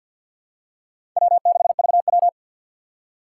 Morse code, heard as a keyed tone, about 53 wpm, 700 Hz. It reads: W6VW
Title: USA Call Signs - 4x Character Spacing - 50wpm